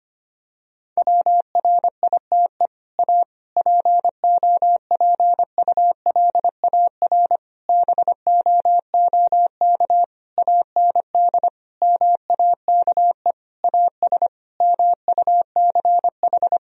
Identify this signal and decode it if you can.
Morse code, heard as a keyed tone, about 25 wpm, 700 Hz. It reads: WRITE A POPULAR BOOK AND MAKE AS MUCH
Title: Sentences from Top 1000 Words - Review Part 3 - 25wpm